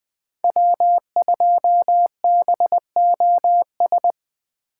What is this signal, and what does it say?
Morse code, heard as a keyed tone, about 21 wpm, 700 Hz. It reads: W2BOS
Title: Speed Racing - USA 1x3 Call Signs 17wpm